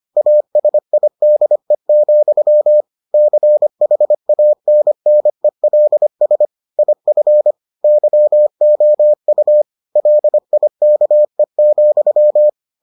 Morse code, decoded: ASIDE, CHANNELS IF YOU LIKE,

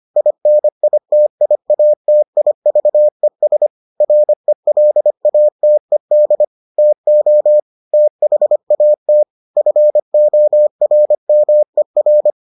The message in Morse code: INITIATIVES RELATED TO THAT FORMER

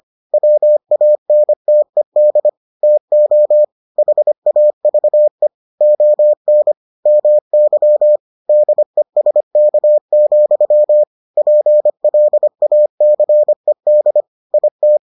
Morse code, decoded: WANTED TO HAVE ON MY DESK, PLACED IT

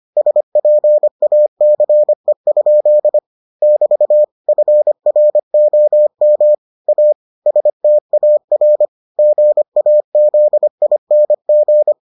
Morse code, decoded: SPACE? <BT> FROM A STAR GAZING

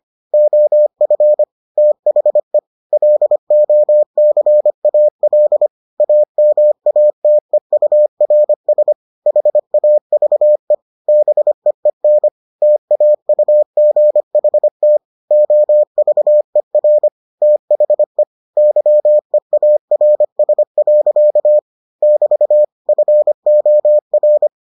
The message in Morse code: OF THE LOCAL AMATEURS HAVE BEEN TAUGHT OVER THE YEARS. <BT> FOR

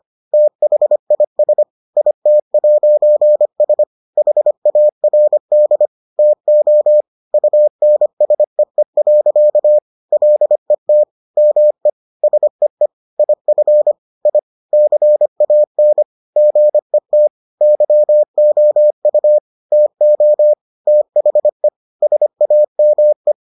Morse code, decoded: THIS IT'S HARD TO UNSEE. LET ME SEE IF I CAN GET YOU TO THE SAME

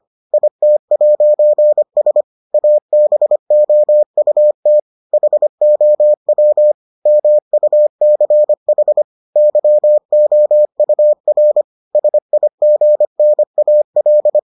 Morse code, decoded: IT'S ABOUT HOW MUCH YOUR SIGNAL

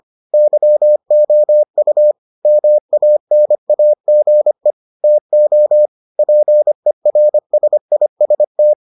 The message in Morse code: YOU MANAGE TO PERSIST

